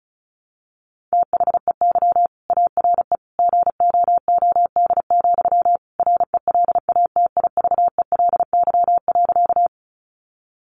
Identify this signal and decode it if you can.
Morse code, heard as a keyed tone, about 34 wpm, 700 Hz. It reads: THEY ARE GOOD, RELATIVELY.
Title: Sentences from Top 1100 Words 35wpm